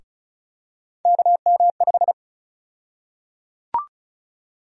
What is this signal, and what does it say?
Morse code, heard as a keyed tone, about 34 wpm, 700 Hz. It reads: KM5
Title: USA Call Sign Prefixes - 35wpm